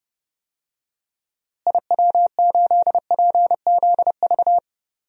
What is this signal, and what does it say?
Morse code, heard as a keyed tone, about 30 wpm, 700 Hz. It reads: IW8PZV
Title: DXCC-100 Call Signs - 6 Characters - Repeated 3x - 30wpm